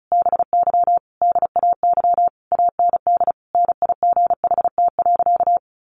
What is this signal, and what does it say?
Morse code, heard as a keyed tone, about 34 wpm, 700 Hz. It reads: BY DAY AND NIGHT.
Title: Sentences from Top 1500 Words 35wpm